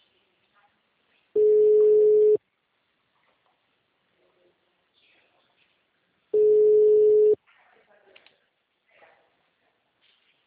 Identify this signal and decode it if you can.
Ringback tone (Europe, 425 Hz), with speech.